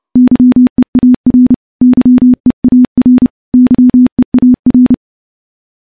Morse code, heard as a keyed tone, about 30 wpm, 250 Hz. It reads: YEAR YEAR YEAR
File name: morse / morse-training-30WPM